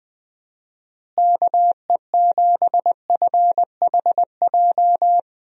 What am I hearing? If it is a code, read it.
Morse code, heard as a keyed tone, about 20 wpm, 700 Hz. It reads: KE7FHJ
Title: USA Call Signs - 6 Characters - Repeated 2x - 20wpm